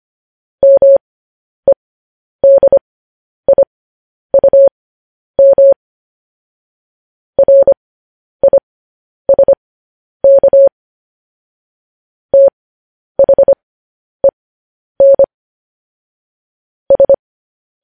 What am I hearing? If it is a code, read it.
Morse code, heard as a keyed tone, about 25 wpm, spaced as for 10 wpm, 550 Hz. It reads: MEDIUM RISK THEN S
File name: podcast / News headlines-1x-25x10